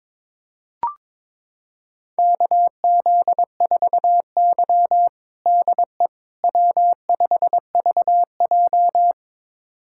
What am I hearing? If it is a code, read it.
Morse code, heard as a keyed tone, about 22 wpm, 700 Hz. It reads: KZ4Y DE W5VJ
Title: USA Callsign DE Callsign - 4 Characters - 22wpm